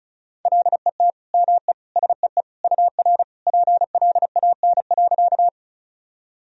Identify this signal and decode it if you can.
Morse code, heard as a keyed tone, about 36 wpm, 700 Hz. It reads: LET ME SEE UR PLAN.
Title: Sentences from Top 500 Words and Abbreviations 35wpm